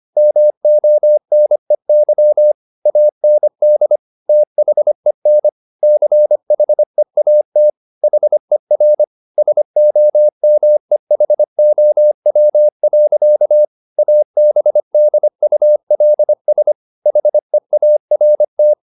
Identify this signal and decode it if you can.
Morse code, heard as a keyed tone, about 25 wpm, 600 Hz. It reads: MONEY AND THEN CHEAT HER SOMEHOW. ABDULS HEART